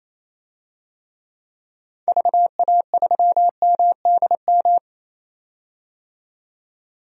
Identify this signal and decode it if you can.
Morse code, heard as a keyed tone, about 28 wpm, 700 Hz. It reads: VA3MDM
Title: DXCC-100 Call Signs - 6 Characters - 28wpm